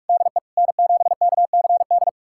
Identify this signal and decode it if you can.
Morse code, heard as a keyed tone, about 46 wpm, 700 Hz. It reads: DE N7KCD
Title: USA Callsign DE Callsign - 5 Characters - 45wpm